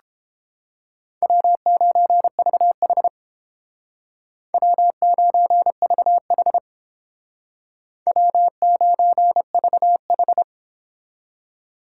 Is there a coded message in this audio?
Yes: Morse code from 1.2 to 10.4 s, heard as a keyed tone, about 31 wpm, 700 Hz. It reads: W9VH W9VH W9VH